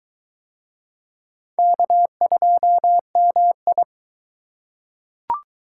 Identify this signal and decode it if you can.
Morse code, heard as a keyed tone, about 23 wpm, 700 Hz. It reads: K2MI